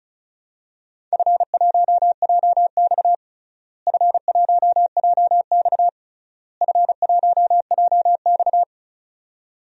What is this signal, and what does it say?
Morse code, heard as a keyed tone, about 34 wpm, 700 Hz. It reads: F1JX F1JX F1JX
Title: DXCC-100 Call Signs - 4 Characters - Repeated 3x - 35wpm